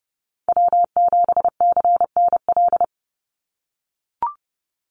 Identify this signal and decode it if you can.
Morse code, heard as a keyed tone, about 30 wpm, 700 Hz. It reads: W7CNL
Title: NA SOTA Chaser Callsigns 30wpm